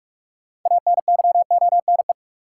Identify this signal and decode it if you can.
Morse code, heard as a keyed tone, about 44 wpm, 700 Hz. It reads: ANYONE
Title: Sentences from Top 900 Words - New Word Drill - 45wpm